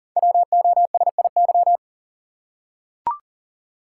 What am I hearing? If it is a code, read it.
Morse code, heard as a keyed tone, about 38 wpm, 690 Hz. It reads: WOSIY
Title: Code Groups - 5 Letter:Numbers - 40wpm